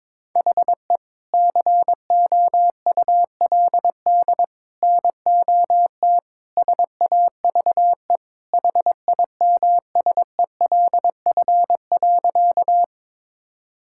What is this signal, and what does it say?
Morse code, heard as a keyed tone, about 22 wpm, 700 Hz. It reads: HE COULD NOT SAVE HIMSELF.
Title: Sentences from Top 1600 Words 22wpm